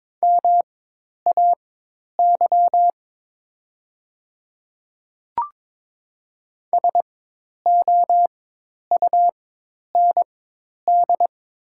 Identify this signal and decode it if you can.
Morse code, heard as a keyed tone, about 22 wpm, spaced as for 10 wpm, 700 Hz. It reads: MAY SOUND
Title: 4x Character Spacing - Top 100 Words - 22wpm